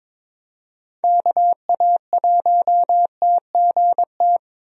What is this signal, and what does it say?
Morse code, heard as a keyed tone, about 22 wpm, 700 Hz. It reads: KA1TGT
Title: USA Call Signs - 6 Characters - 22wpm